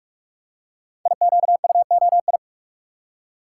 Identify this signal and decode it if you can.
Morse code, heard as a keyed tone, about 44 wpm, 700 Hz. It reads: IQUOI
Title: Code Groups - 5 Letters - 45wpm